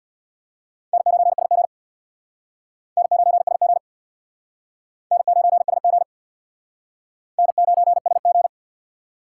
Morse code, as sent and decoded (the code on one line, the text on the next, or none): -. ----. ... --. / -. ----. ... --. / -. ----. ... --. / -. ----. ... --.
N9SG N9SG N9SG N9SG